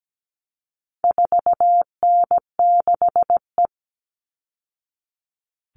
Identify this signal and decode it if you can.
Morse code, heard as a keyed tone, about 17 wpm, 700 Hz. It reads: 4N6E
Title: All Cut Numbers - 4-Digit 17wpm